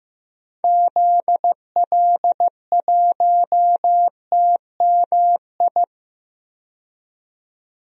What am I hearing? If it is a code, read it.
Morse code, heard as a keyed tone, about 15 wpm, 690 Hz. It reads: ZL1TMI
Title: DXCC-100 Call Signs - 6 Characters - 15wpm